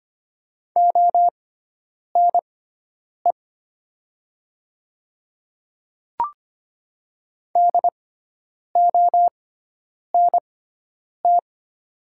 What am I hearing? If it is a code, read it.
Morse code, heard as a keyed tone, about 25 wpm, spaced as for 9 wpm, 700 Hz. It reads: ONE DONT
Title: 6x Character Spacing - Top 300 Words - 25wpm